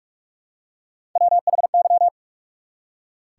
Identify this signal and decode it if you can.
Morse code, heard as a keyed tone, about 45 wpm, 700 Hz. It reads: WHY
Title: Three Letter Words - 45wpm